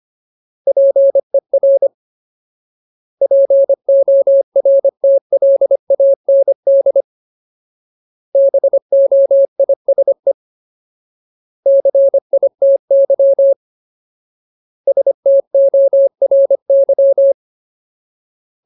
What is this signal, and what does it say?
Morse code, heard as a keyed tone, about 25 wpm, 550 Hz. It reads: PER PORTLAND BOISE CITY STORY
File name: Ultimate Word List / 25WPM Ultimate